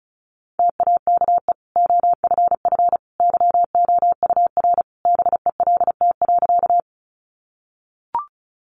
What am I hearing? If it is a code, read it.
Morse code, heard as a keyed tone, about 34 wpm, 700 Hz. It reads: TAKE OFF YOUR BELT.
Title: Sentences from Top 1400 Words 35wpm